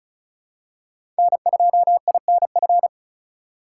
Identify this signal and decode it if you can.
Morse code, heard as a keyed tone, about 36 wpm, 700 Hz. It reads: N2INF
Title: USA Call Signs - 5 Characters - 2x Character Spacing - 35wpm